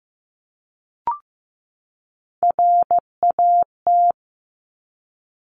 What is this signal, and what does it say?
Morse code, heard as a keyed tone, about 15 wpm, 700 Hz. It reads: RAT
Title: Top Three Letter Combinations in Words 15wpm